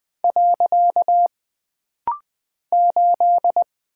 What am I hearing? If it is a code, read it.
Morse code, heard as a keyed tone, about 20 wpm, 700 Hz. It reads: . 8
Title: Single Character (Letter:Number:Punctuation:Prosign) - Rapid-Fire - 20wpm